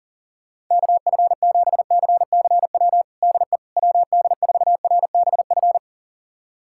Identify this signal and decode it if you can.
Morse code, heard as a keyed tone, about 40 wpm, 700 Hz. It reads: KF7CCW DE WD4RBF
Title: USA Callsign DE Callsign - 6 Characters - 40wpm